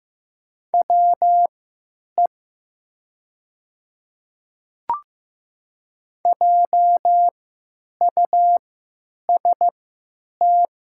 Morse code, heard as a keyed tone, about 15 wpm, spaced as for 9 wpm, 700 Hz. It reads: WE JUST
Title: 3x Character Spacing - Top 500 Words - 15wpm